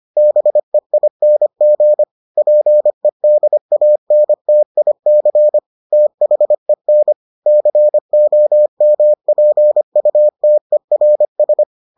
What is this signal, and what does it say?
Morse code, heard as a keyed tone, about 25 wpm, 600 Hz. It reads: BEING PEDANTIC THEN COMPUTERS